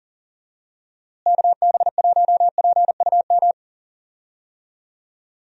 Morse code, heard as a keyed tone, about 40 wpm, 700 Hz. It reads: KB1PUM
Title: USA Call Signs - 6 Characters - Repeated 2x - 40wpm